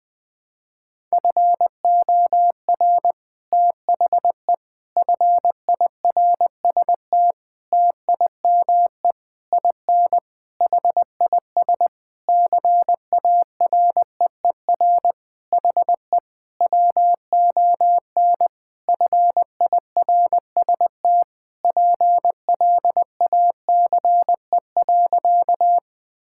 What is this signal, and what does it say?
Morse code, heard as a keyed tone, about 20 wpm, 700 Hz. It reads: FOR THE FIRST TIME IN HIS CAREER HE WON FIRST PLACE.
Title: Sentences with 7-Word Phrases - 20wpm